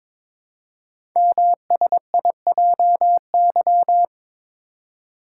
Morse code, heard as a keyed tone, about 22 wpm, 700 Hz. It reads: MSIJY
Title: Code Groups - 5 Letters - 22wpm